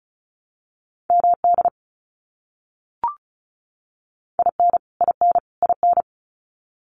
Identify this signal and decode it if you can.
Morse code, heard as a keyed tone, about 34 wpm, 700 Hz. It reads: MD IN IN IN
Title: US States Repeated 3x 35wpm